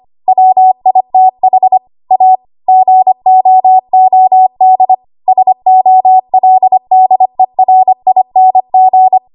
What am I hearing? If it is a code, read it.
Morse code, heard as a keyed tone, about 25 wpm, 750 Hz. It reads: WITH A GOOD SOLDERING
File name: Archive / 25 WPM / 25WPM